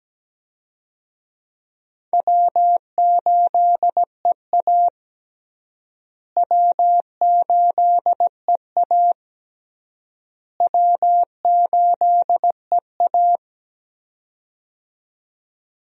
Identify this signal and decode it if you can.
Morse code, heard as a keyed tone, about 17 wpm, 700 Hz. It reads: W8EA W8EA W8EA
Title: USA Call Signs - 4 Characters - Repeated 3x - 17wpm